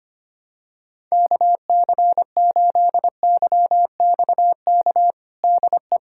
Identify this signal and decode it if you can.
Morse code, heard as a keyed tone, about 25 wpm, 700 Hz. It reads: KC8YXK DE
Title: USA Callsign DE Callsign - 6 Characters - 25wpm